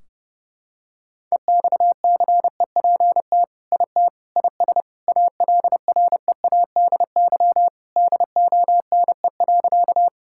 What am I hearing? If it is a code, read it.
Morse code, heard as a keyed tone, about 30 wpm, 700 Hz. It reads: EXCEPT IT IS ALREADY DONE.
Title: Sentences from Top 1600 Words 30wpm